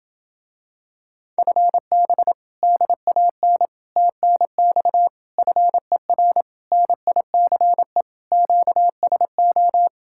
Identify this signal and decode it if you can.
Morse code, heard as a keyed tone, about 28 wpm, 700 Hz. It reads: FB DAN TNX FER NICE QSO